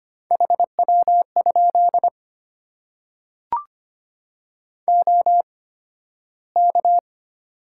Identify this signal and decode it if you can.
Morse code, heard as a keyed tone, about 25 wpm, spaced as for 13 wpm, 700 Hz. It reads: HW? OK
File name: Top 90 QSO Elements - 8x Character Spacing - 25wpm